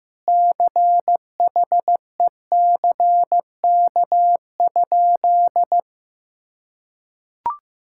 Morse code, heard as a keyed tone, about 15 wpm, 700 Hz. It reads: CHECK?
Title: Sentences from Top 1000 Words - Review Part 6 - 15wpm